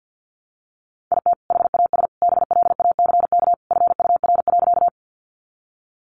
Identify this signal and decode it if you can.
Morse code, heard as a keyed tone, about 50 wpm, 700 Hz. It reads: IT HAS BLACK FUR.
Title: Easy Phrases 50wpm